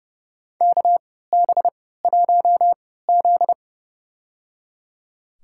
Morse code, heard as a keyed tone, about 29 wpm, spaced as for 17 wpm, 700 Hz. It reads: KB1Z
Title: USA Call Signs - 3x Character Spacing - 30wpm